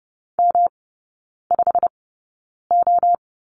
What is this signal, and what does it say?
Morse code, heard as a keyed tone, about 30 wpm, spaced as for 9 wpm, 700 Hz. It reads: M5O
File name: Lesson 18e - Letter m - Three Characters 30wpm